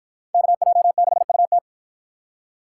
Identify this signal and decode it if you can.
Morse code, heard as a keyed tone, about 53 wpm, 700 Hz. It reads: KJ6UT